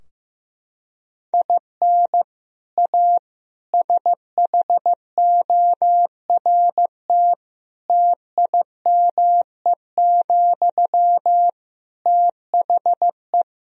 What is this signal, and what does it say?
Morse code, heard as a keyed tone, about 15 wpm, 700 Hz. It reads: IN A SHORT TIME, THE
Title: Sentences from Top 1000 Words - Review Part 2 - 15wpm